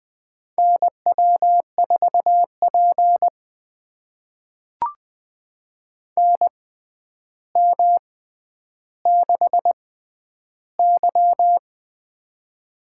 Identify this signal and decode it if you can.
Morse code, heard as a keyed tone, about 20 wpm, spaced as for 10 wpm, 700 Hz. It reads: NW4P NM6Y
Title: USA Call Signs - 4 Characters - 6x Character Spacing - 20wpm